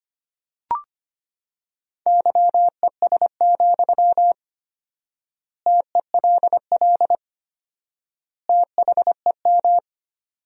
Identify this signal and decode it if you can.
Morse code, heard as a keyed tone, about 25 wpm, 700 Hz. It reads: YES, TELL THEM
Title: 4x Word-Spacing - Sentences from Top 500 Words - 25wpm